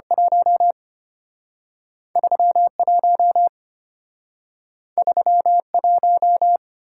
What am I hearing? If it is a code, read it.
Morse code, heard as a keyed tone, about 34 wpm, 700 Hz. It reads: 1 31 31